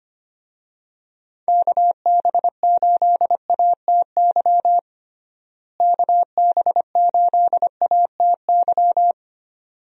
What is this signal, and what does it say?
Morse code, heard as a keyed tone, about 25 wpm, 700 Hz. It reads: KB8ATY KB8ATY
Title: USA Call Signs - 6 Characters - Repeated 2x - 25wpm